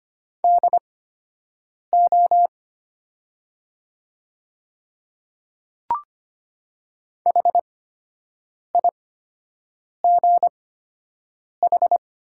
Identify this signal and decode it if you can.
Morse code, heard as a keyed tone, about 25 wpm, spaced as for 7 wpm, 700 Hz. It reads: DO HIGH
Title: 8x Character Spacing - Top 200 Words - 25wpm